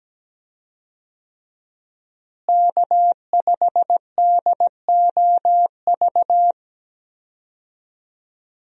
Morse code, heard as a keyed tone, about 17 wpm, 700 Hz. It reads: K5DOV
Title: USA Call Signs - 5 Characters - Repeated 3x - 17wpm